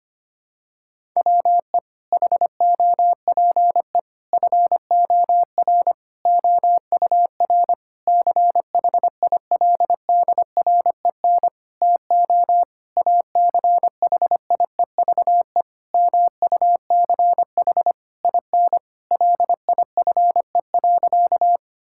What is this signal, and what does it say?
Morse code, heard as a keyed tone, about 25 wpm, 700 Hz. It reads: WE HOPE FOR OUR CHILDREN TO ACHIEVE MUCH IN LIFE.
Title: Sentences from Top 1000 Words - Review Part 6 - 25wpm